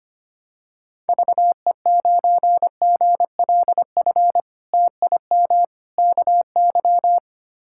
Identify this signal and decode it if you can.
Morse code, heard as a keyed tone, about 25 wpm, 700 Hz. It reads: VE9GLF TIM KY
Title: K1USN SST QSO Practice - 25wpm